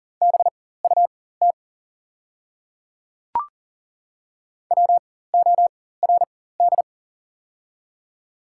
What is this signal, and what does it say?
Morse code, heard as a keyed tone, about 40 wpm, spaced as for 19 wpm, 700 Hz. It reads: BUT WORD
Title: 4x Character Spacing - Top 100 Words - 40wpm